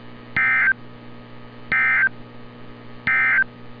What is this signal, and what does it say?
Emergency Alert System (SAME) tones, heard as harsh buzzing bursts. It is heard from 0.2 to 3.5 s.
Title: NOAA Weather Radio - Altoona, PA